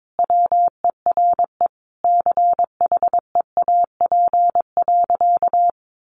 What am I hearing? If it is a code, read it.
Morse code, heard as a keyed tone, about 22 wpm, 700 Hz. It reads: WERE CHEAP.
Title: Sentences from Top 1200 Words 22wpm